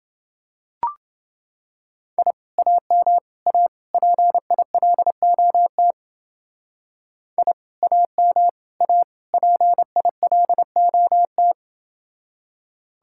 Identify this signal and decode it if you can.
Morse code, heard as a keyed tone, about 28 wpm, 700 Hz. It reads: I AM A PILOT I AM A PILOT